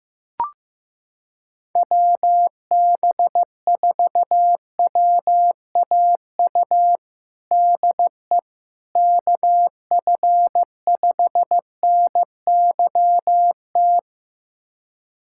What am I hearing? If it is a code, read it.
Morse code, heard as a keyed tone, about 15 wpm, 700 Hz. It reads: WB4WAU DE KF5NYT
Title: USA Callsign DE Callsign - 6 Characters - 15wpm